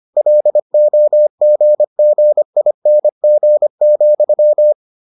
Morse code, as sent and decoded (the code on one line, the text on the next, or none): .-.. --- --. --. .. -. --. --..--
LOGGING,